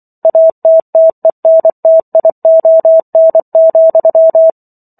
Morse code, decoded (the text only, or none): ATTENTION,